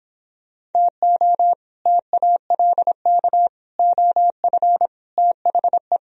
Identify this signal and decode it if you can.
Morse code, heard as a keyed tone, about 29 wpm, 700 Hz. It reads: TO TALK OF THE